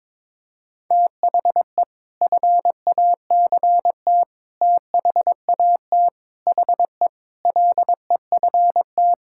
Morse code, heard as a keyed tone, about 22 wpm, 700 Hz. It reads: THE FACT THAT HE LEFT
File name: practice / 22wpm / Sentences with 2-Word Phrases - 22wpm